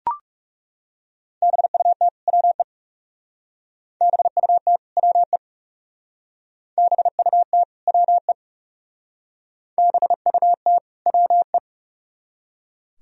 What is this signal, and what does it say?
Morse code, heard as a keyed tone, about 37 wpm, 700 Hz. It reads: BUT WE BUT WE BUT WE BUT WE
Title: Speed-Racing - Sets of 2 Words 30wpm